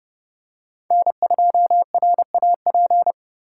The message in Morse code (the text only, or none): N2RAP